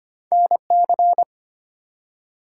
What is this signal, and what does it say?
Morse code, heard as a keyed tone, about 25 wpm, 700 Hz. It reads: NC
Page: morse-code-ninja-practice-25wpm